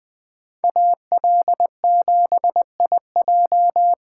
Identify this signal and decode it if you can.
Morse code, heard as a keyed tone, about 21 wpm, 700 Hz. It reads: AL7IJ